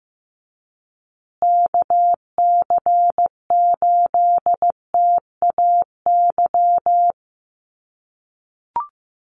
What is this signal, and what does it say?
Morse code, heard as a keyed tone, about 15 wpm, 700 Hz. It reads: KC8TAY
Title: USA Call Signs Repeated 2x 15wpm